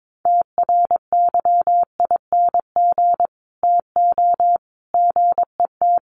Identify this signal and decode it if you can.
Morse code, heard as a keyed tone, about 22 wpm, 700 Hz. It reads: TRYING TO GET
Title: Sets of 3 Words 22wpm